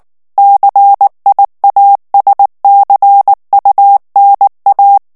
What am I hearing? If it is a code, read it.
Morse code, heard as a keyed tone, about 19 wpm, 800 Hz. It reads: CIASCUNA
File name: morse mp3 / ITALIEN texte clair 2 vitesse 18 mots minute